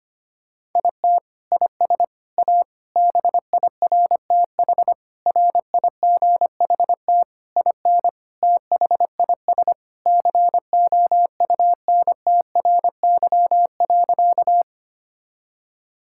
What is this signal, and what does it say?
Morse code, heard as a keyed tone, about 25 wpm, 700 Hz. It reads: IT IS A BIRTH RIGHT IN THIS COUNTRY.
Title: Sentences from Top 1000 Words - Review Part 5 - 25wpm